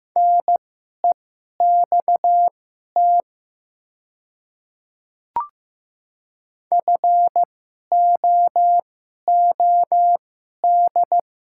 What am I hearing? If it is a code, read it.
Morse code, heard as a keyed tone, about 15 wpm, spaced as for 11 wpm, 700 Hz. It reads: NEXT FOOD
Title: 2x Character Spacing - Top 300 Words - 15wpm